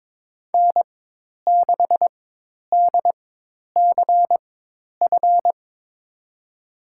Morse code, heard as a keyed tone, about 22 wpm, spaced as for 10 wpm, 700 Hz. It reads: N6DCF
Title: USA Call Signs - 5 Characters - 4x Character Spacing - 22wpm